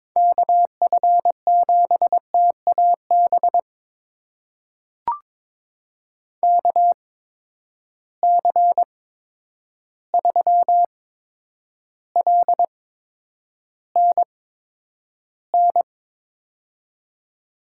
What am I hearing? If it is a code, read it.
Morse code, heard as a keyed tone, about 23 wpm, spaced as for 10 wpm, 700 Hz. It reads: KF7TAB KC3LNN
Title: USA Call Signs - 6 Characters - 8x Character Spacing - 22wpm